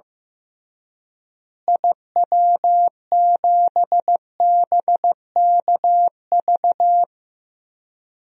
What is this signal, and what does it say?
Morse code, heard as a keyed tone, about 15 wpm, 700 Hz. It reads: IW7BKV
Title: DXCC-100 Call Signs - 6 Characters - Repeated 3x - 15wpm